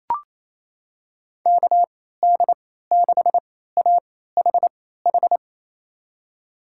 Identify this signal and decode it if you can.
Morse code, heard as a keyed tone, about 28 wpm, spaced as for 16 wpm, 700 Hz. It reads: KD6AHH
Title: USA Call Signs - 3x Character Spacing - 28wpm